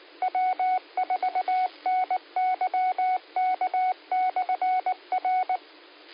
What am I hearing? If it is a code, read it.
Morse code, heard as a keyed tone, about 19 wpm, 740 Hz. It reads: W4NYK/R